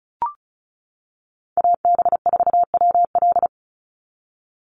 Morse code, heard as a keyed tone, about 34 wpm, 700 Hz. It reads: AB4WL
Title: NA SOTA Chaser Callsigns 35wpm